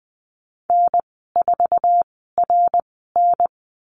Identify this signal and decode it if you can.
Morse code, heard as a keyed tone, about 20 wpm, spaced as for 14 wpm, 700 Hz. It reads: N4RN
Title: 062 Lesson 11g - Letter h - Call Signs 20wpm